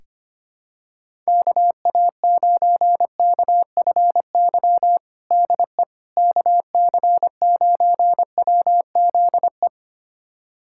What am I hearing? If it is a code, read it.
Morse code, heard as a keyed tone, about 25 wpm, 700 Hz. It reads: KA9KFY DE KC9WZE
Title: USA Callsign DE Callsign - 6 Characters - 25wpm